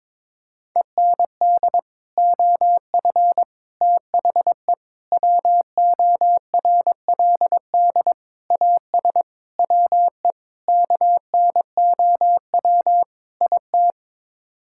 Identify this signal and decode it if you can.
Morse code, heard as a keyed tone, about 22 wpm, 700 Hz. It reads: END OF THE WORLD AS WE KNOW IT
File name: practice / 22wpm / Sets of 8 Words - 22wpm